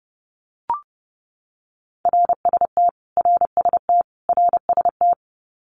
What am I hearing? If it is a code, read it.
Morse code, heard as a keyed tone, about 30 wpm, 700 Hz. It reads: RST RST RST
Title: Abbreviations Repeated 3x - 30wpm